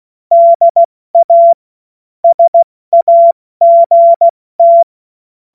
Morse code, heard as a keyed tone, about 16 wpm, spaced as for 14 wpm, 680 Hz. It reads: DA SAGT
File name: Morsekurs / Witze / Witz 28. September 2021, 70 F80